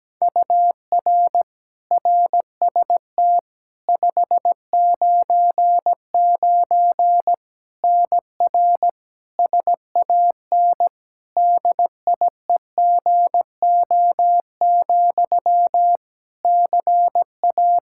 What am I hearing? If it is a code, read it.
Morse code, heard as a keyed tone, about 17 wpm, 700 Hz. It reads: UR RST 599 NR SAN DIEGO, CA
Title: Practice QSO - Exchange 1 - Beginning - 17wpm